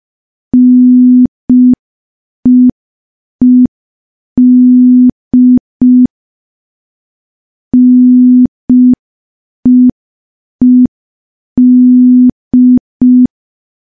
Morse code, heard as a keyed tone, about 5 wpm, 250 Hz. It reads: NEED NEED
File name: morse / morse-training-05WPM